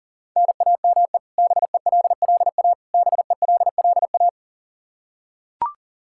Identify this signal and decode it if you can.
Morse code, heard as a keyed tone, about 40 wpm, 700 Hz. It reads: NAME BELLA BELLA
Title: Practice QSO - Exchange 1 - Beginning - 40wpm